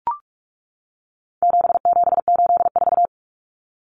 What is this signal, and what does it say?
Morse code, heard as a keyed tone, about 44 wpm, 700 Hz. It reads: 7784
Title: Common Cut Numbers - 4-Digit 45wpm